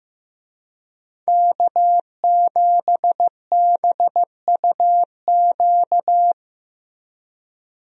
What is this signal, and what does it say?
Morse code, heard as a keyed tone, about 16 wpm, 700 Hz. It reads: K7BUQ
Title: USA Call Signs - 5 Characters - Repeated 2x - 15wpm